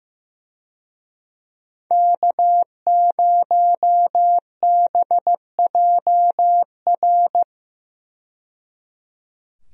Morse code, heard as a keyed tone, about 15 wpm, 700 Hz. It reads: K0BJR
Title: USA Call Signs - 5 Characters - 15wpm